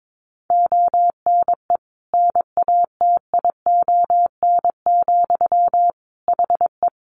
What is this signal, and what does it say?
Morse code, heard as a keyed tone, about 22 wpm, 700 Hz. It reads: ONE NATION, HE
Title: Sentences from Top 900 Words 22wpm